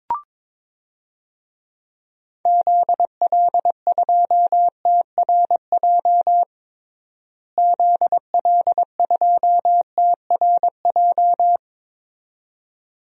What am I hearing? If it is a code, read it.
Morse code, heard as a keyed tone, about 22 wpm, 700 Hz. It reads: ZL2TRJ ZL2TRJ
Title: DXCC-100 Call Signs - 6 Characters - Repeated 2x - 22wpm